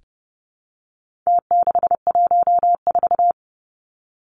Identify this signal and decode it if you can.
Morse code, heard as a keyed tone, about 30 wpm, 700 Hz. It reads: T614
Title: Numbers - Contest Style - 4-Digit 30wpm